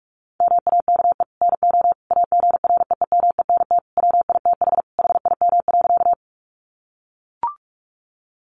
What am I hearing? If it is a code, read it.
Morse code, heard as a keyed tone, about 44 wpm, 700 Hz. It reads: MAKE NO AGREEMENT WITH HIM.
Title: Sentences from Top 1500 Words 45wpm